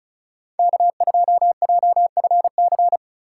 Morse code, heard as a keyed tone, about 36 wpm, 700 Hz. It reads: K2JFC
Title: USA Call Signs - 3x Character Spacing - 35wpm